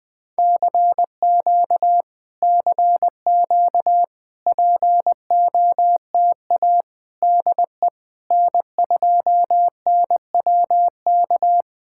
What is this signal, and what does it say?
Morse code, heard as a keyed tone, about 20 wpm, 690 Hz. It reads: CQ CQ POTA DE N2NWK
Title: POTA QSO Practice - 20wpm